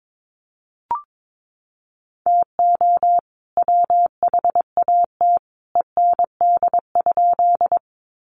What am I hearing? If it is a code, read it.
Morse code, heard as a keyed tone, about 22 wpm, 700 Hz. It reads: TO WHAT END?
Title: Sentences from Top 200 Words 22wpm